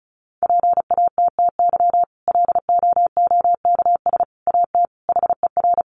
Morse code, heard as a keyed tone, about 34 wpm, 700 Hz. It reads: PATTY LOOKS AT HER
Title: Additional Easy Phrases 35wpm